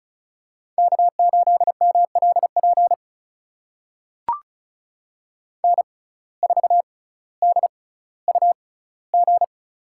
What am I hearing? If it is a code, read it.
Morse code, heard as a keyed tone, about 36 wpm, spaced as for 18 wpm, 700 Hz. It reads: K8MLP N4DUG
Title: USA Call Signs - 5 Characters - 6x Character Spacing - 35wpm